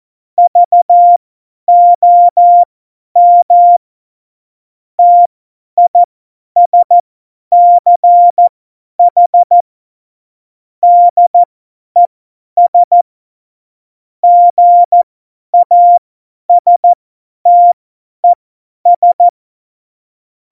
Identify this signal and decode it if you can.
Morse code, heard as a keyed tone, about 14 wpm, spaced as for 10 wpm, 700 Hz. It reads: VOM TISCH DES GASTES
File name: Morsekurs / Witze / Witz 14. Januar 2020, 50 F70